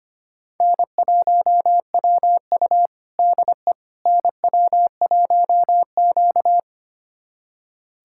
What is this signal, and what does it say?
Morse code, heard as a keyed tone, about 25 wpm, 700 Hz. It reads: N1WU DE NW1Q
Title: USA Callsign DE Callsign - 4 Characters - 25wpm